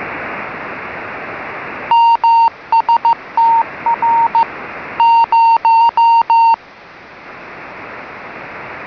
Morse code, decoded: MSTR 0